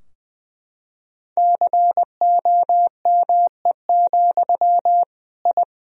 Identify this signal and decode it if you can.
Morse code, heard as a keyed tone, about 20 wpm, 700 Hz. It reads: COME, I
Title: Sentences from Top 1000 Words - Review Part 2 - 20wpm